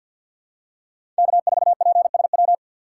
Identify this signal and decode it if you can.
Morse code, heard as a keyed tone, about 53 wpm, 700 Hz. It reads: K4PSW
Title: USA Call Signs - 5 Characters - 2x Character Spacing - 50wpm